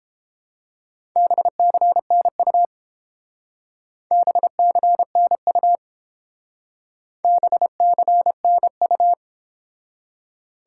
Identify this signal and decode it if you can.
Morse code, heard as a keyed tone, about 30 wpm, 700 Hz. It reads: BCNU BCNU BCNU